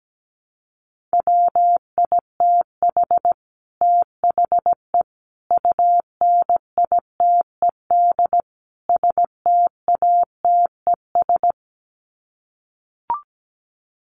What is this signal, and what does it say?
Morse code, heard as a keyed tone, about 17 wpm, 700 Hz. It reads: WITH THE UNITED STATES
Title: Sets of 4 Words 17wpm